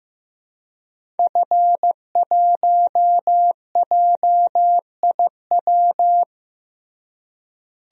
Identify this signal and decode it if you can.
Morse code, heard as a keyed tone, about 15 wpm, 700 Hz. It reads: F1JIW
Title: DXCC-100 Call Signs - 5 Characters - Repeated 3x - 15wpm